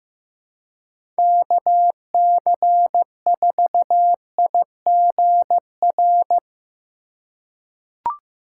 Morse code, heard as a keyed tone, about 15 wpm, 700 Hz. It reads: KC4IGR
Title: USA Call Signs - 6 Characters - 2x Character Spacing - 15wpm